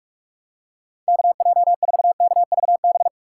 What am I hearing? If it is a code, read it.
Morse code, heard as a keyed tone, about 44 wpm, 700 Hz. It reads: KJ4KVB
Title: USA Call Signs - 6 Characters - 2x Character Spacing - 45wpm